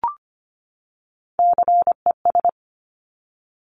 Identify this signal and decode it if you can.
Morse code, heard as a keyed tone, about 25 wpm, 700 Hz. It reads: CES
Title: Top Three Letter Combinations in Words 25wpm